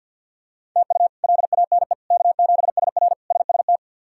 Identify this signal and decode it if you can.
Morse code, heard as a keyed tone, about 46 wpm, 700 Hz. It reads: TU LANE K7SR SST
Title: K1USN SST QSO Practice - 50wpm